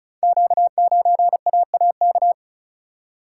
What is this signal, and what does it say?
Morse code, heard as a keyed tone, about 36 wpm, 690 Hz. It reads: Q9AAK
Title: Code Groups - 5 Letter:Numbers - 35wpm